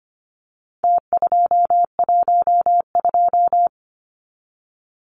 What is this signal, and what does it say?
Morse code, heard as a keyed tone, about 25 wpm, 700 Hz. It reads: T212
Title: Numbers - Contest Style - 4-Digit 25wpm